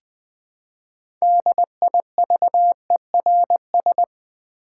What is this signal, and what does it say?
Morse code, heard as a keyed tone, about 20 wpm, 700 Hz. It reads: DIVERS
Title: Build A Word - 3 Letters At A Time - 9-Letter Words - 20wpm